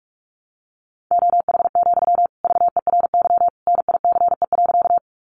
Morse code, heard as a keyed tone, about 44 wpm, 700 Hz. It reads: OH, VERY NICE.
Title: Sentences from Top 1100 Words 45wpm